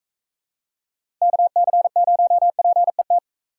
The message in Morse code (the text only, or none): KC0PET